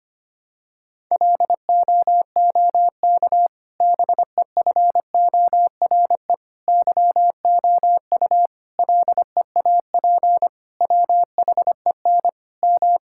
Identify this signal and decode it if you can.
Morse code, heard as a keyed tone, about 25 wpm, 700 Hz. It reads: LOOK BEFORE YOU LEAP WHEN M
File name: practice / 25wpm / Difficult Sentences with Hints - 25wpm